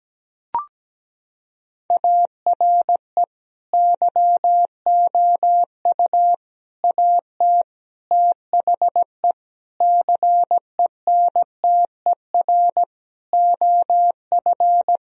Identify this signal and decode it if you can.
Morse code, heard as a keyed tone, about 17 wpm, 700 Hz. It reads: ARE YOU AT THE CENTER OF
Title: Sentences from Top 1000 Words - Review Part 2 - 17wpm